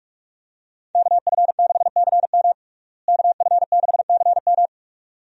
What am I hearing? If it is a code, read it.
Morse code, heard as a keyed tone, about 46 wpm, 700 Hz. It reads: KF6CM KF6CM
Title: USA Call Signs - 5 Characters - Repeated 2x - 45wpm